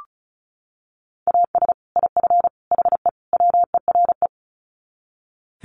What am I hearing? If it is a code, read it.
Morse code, heard as a keyed tone, about 34 wpm, 700 Hz. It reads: AS IF HE WERE
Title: Sets of 4 Words 35wpm